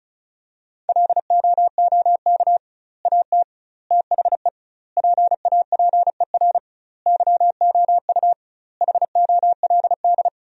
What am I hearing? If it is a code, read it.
Morse code, heard as a keyed tone, about 36 wpm, 700 Hz. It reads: LOOK AT THE PAPER YOU HOLD
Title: 2x Word-Spacing - Sentences from Top 400 Words - 35wpm